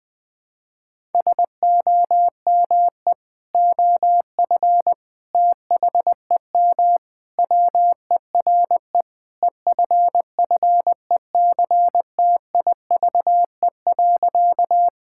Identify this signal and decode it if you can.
Morse code, heard as a keyed tone, about 20 wpm, 700 Hz. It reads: SOME OF THEM WERE EFFECTIVE.
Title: Sentences from Top 1000 Words - Review Part 6 - 20wpm